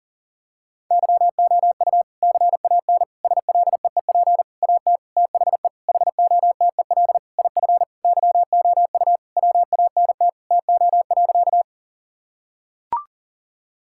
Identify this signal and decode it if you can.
Morse code, heard as a keyed tone, about 40 wpm, 700 Hz. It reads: YOU CAN SLEEP AT THE HOTEL IF YOU WANT TO.
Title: Sentences from Top 1600 Words 40wpm